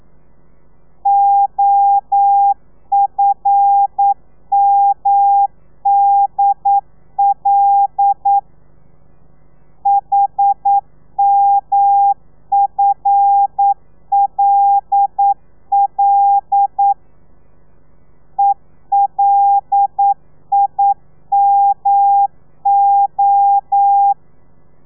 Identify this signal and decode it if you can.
Morse code, heard as a keyed tone, about 9 wpm, 800 Hz. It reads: OFMDL HMFLL ELIMO
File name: telegrafia / i2tfj / Lez5 40